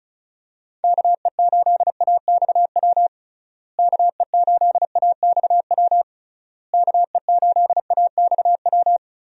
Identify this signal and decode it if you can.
Morse code, heard as a keyed tone, about 34 wpm, 700 Hz. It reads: KE8AXW KE8AXW KE8AXW
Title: USA Call Signs - 6 Characters - Repeated 3x - 35wpm